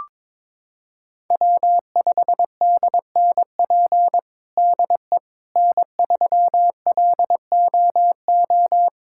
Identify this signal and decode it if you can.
Morse code, heard as a keyed tone, about 22 wpm, 700 Hz. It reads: W5DNP DE N3LOO
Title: USA Callsign DE Callsign - 5 Characters - 22wpm